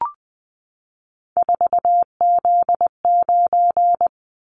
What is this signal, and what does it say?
Morse code, heard as a keyed tone, about 20 wpm, 700 Hz. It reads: 4Z9
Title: Callsign Prefixes for Top 50 DXCC - 20wpm